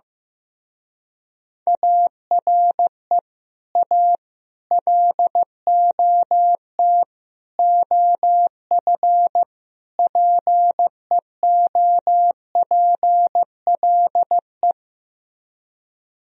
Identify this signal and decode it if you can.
Morse code, heard as a keyed tone, about 15 wpm, 700 Hz. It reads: ARE A LOT OF PEOPLE
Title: Speed-Racing - Sets of 5 Words 15wpm